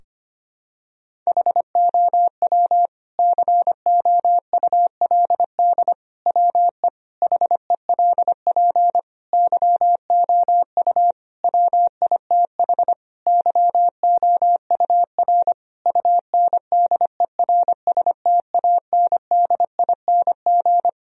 Morse code, heard as a keyed tone, about 25 wpm, 700 Hz. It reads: HOW COULD WE HELP YOU WITH YOUR UNDERSTANDING
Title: Sentences from Top 1000 Words - Review Part 3 - 25wpm